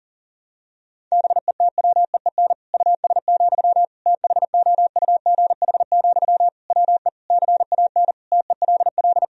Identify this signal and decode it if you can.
Morse code, heard as a keyed tone, about 40 wpm, 700 Hz. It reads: BETWEEN US, THOUGH, WE CAN TELL
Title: Sentences from Top 1000 Words - Review Part 2 - 40wpm